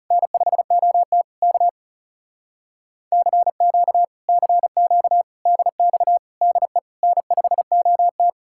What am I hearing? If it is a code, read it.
Morse code, heard as a keyed tone, about 40 wpm, 700 Hz. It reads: N5OT K CQ CQ DX DE N5OT